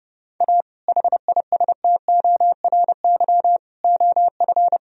Morse code, heard as a keyed tone, about 29 wpm, 700 Hz. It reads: A HISTORY OF
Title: Sentences from Top 1000 Words - Review Part 3 - 30wpm